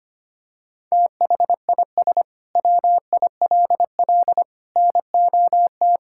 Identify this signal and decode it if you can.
Morse code, heard as a keyed tone, about 25 wpm, 700 Hz. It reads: THIS WILL NOT
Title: Sentences from Top 1000 Words - Review Part 6 - 25wpm